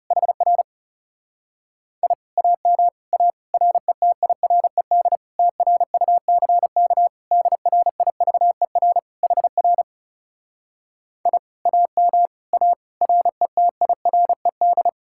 Morse code, heard as a keyed tone, about 34 wpm, 700 Hz. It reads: HR I AM A RETIRED TRUCK DRIVER HR I AM A RETIRED
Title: Speed-Racing Common QSO Phrases Encore - 30wpm